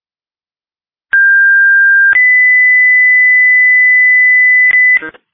Two-tone page, heard as a one-second tone, then a three-second tone, about 1600 then 2000 Hz, with noise. It is heard from 1.1 to 5.0 s.